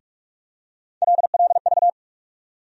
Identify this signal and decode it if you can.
Morse code, heard as a keyed tone, about 46 wpm, 700 Hz. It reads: LLV
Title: USA Call Sign Postfixes - 3 Letters 45wpm